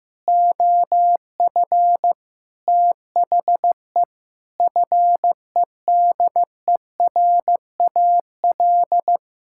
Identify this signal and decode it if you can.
Morse code, heard as a keyed tone, about 15 wpm, 700 Hz. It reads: OF THE FEDERAL
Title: Sentences from Top 1000 Words - Review Part 5 - 15wpm